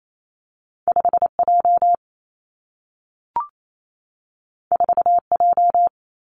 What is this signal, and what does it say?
Morse code, heard as a keyed tone, about 28 wpm, 700 Hz. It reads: 5 J 4 J
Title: Two Challenging Letter-Number Combinations 28wpm